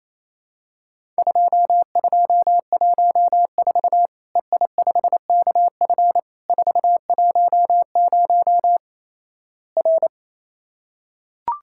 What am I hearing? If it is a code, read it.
Morse code, heard as a keyed tone, about 28 wpm, 700 Hz. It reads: 2214 EI5KF 410 R
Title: Worked All Europe DX Contest - QTC Exchange - 28wpm